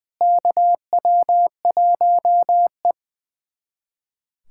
Morse code, heard as a keyed tone, about 20 wpm, 700 Hz. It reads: KW1E
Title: USA Callsign DE Callsign - 4 Characters - 20wpm